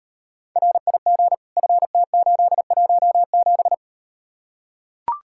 Morse code, heard as a keyed tone, about 38 wpm, 700 Hz. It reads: RIG FT817